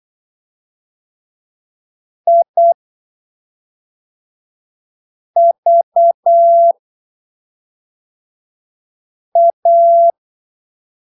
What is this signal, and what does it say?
Morse code, heard as a keyed tone, about 8 wpm, spaced as for 3 wpm, 680 Hz. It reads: IVA
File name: arquivos / texto / texto 19 08ppm